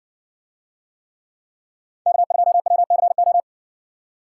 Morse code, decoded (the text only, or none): K2WCO